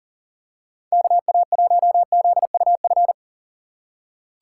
Morse code, heard as a keyed tone, about 40 wpm, 700 Hz. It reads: KA1ZUF
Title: USA Call Signs - 6 Characters - 6x Character Spacing - 40wpm